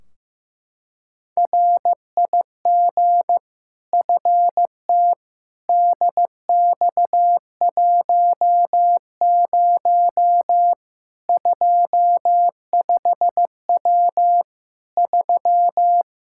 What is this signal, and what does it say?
Morse code, heard as a keyed tone, about 15 wpm, 700 Hz. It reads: RIG FTDX10 25W 3
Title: Practice QSO - Exchange 2 - Info - 15wpm